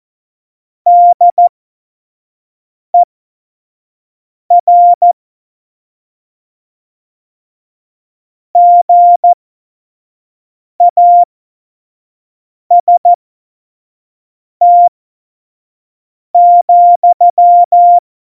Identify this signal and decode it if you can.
Morse code, heard as a keyed tone, about 14 wpm, spaced as for 5 wpm, 700 Hz. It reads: DER GAST,
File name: Morsekurs / Witze / Witz 10. September 2019, 25 F70